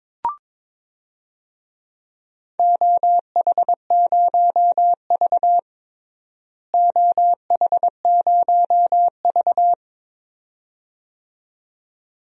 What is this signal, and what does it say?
Morse code, heard as a keyed tone, about 22 wpm, 700 Hz. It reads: OH0V OH0V
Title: DXCC-100 Call Signs - 4 Characters - Repeated 2x - 22wpm